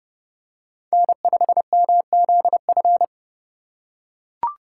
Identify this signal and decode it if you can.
Morse code, heard as a keyed tone, about 30 wpm, 700 Hz. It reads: N5MZF
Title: USA Call Signs - 5 Characters - 4x Character Spacing - 30wpm